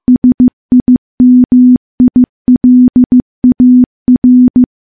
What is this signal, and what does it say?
Morse code, heard as a keyed tone, about 15 wpm, 250 Hz. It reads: SIMILAR
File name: morse-training-15WPM